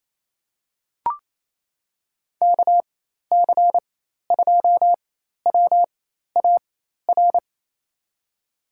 Morse code, heard as a keyed tone, about 28 wpm, spaced as for 13 wpm, 700 Hz. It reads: KC2WAR
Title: USA Call Signs - 4x Character Spacing - 28wpm